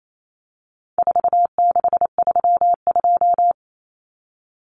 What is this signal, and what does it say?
Morse code, heard as a keyed tone, about 28 wpm, 700 Hz. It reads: 4632
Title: Common Cut Numbers - 4-Digit 28wpm